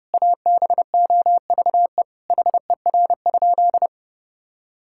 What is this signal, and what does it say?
Morse code, heard as a keyed tone, about 30 wpm, 700 Hz. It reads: ABOVE HER?
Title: Sentences from Top 1000 Words - Review Part 5 - 30wpm